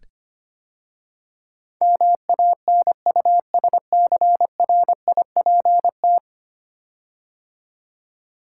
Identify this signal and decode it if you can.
Morse code, heard as a keyed tone, about 25 wpm, 700 Hz. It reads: MANUSCRIPT
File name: 10 Letter Words - First 3 Letters Given - 25wpm